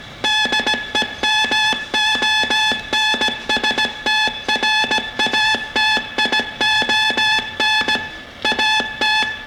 Morse code, heard as a keyed tone, about 17 wpm, 860 Hz. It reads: DEMONSTRATION AT